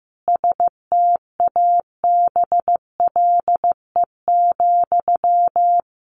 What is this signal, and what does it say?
Morse code, heard as a keyed tone, about 15 wpm, 700 Hz. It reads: STABLE,
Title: Sentences from Top 1500 Words 15wpm